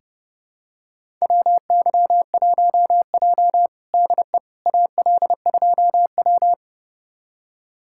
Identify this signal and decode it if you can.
Morse code, heard as a keyed tone, about 30 wpm, 700 Hz. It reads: WY1J DE AL2W
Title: USA Callsign DE Callsign - 4 Characters - 30wpm